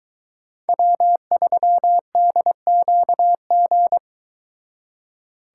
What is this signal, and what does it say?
Morse code, heard as a keyed tone, about 22 wpm, 700 Hz. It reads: W3DQG